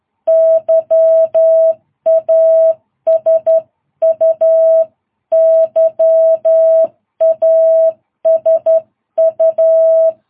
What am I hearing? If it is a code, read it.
Morse code, heard as a keyed tone, about 12 wpm, 660 Hz. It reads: YASU YASU
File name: jh8cbh / 2022cw